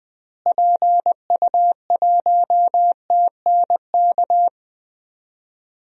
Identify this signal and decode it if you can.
Morse code, heard as a keyed tone, about 21 wpm, 700 Hz. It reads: PU1TNK
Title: DXCC-100 Call Signs - 6 Characters - Repeated 3x - 20wpm